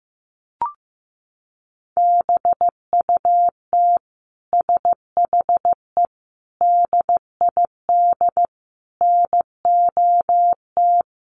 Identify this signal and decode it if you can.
Morse code, heard as a keyed tone, about 15 wpm, 700 Hz. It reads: BUT SHE DID NOT
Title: Sentences from Top 600 Words 15wpm